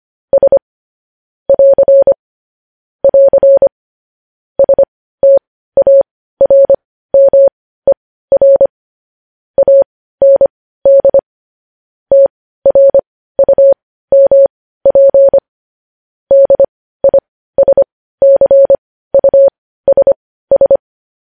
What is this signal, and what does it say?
Morse code, heard as a keyed tone, about 25 wpm, spaced as for 15 wpm, 550 Hz. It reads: S <AR> <AR> STARMER AND TRUMP DISCUSS
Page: news-headlines-in-morse-code-at-15-wpm